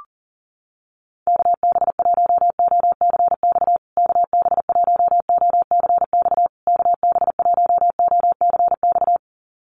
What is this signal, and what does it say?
Morse code, heard as a keyed tone, about 40 wpm, 700 Hz. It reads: KB1OCX KB1OCX KB1OCX
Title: USA Call Signs Repeated 3x 40wpm